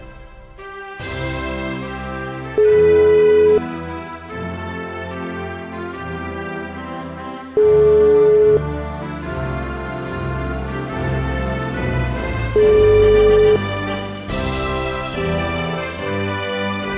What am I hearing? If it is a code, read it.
Ringback tone (Europe, 425 Hz), with noise.